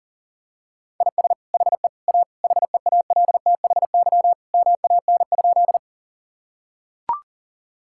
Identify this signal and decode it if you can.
Morse code, heard as a keyed tone, about 40 wpm, 700 Hz. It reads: IS HE A HEALTHY MAN?
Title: Sentences from Top 1000 Words - Review Part 5 - 40wpm